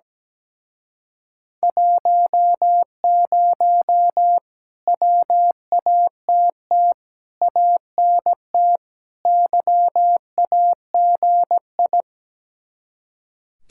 Morse code, heard as a keyed tone, about 17 wpm, 700 Hz. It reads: 10 WATT ANT YAGI
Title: Speed-Racing Common QSO Phrases Encore - 17wpm